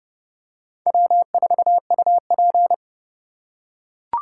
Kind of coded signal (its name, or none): Morse code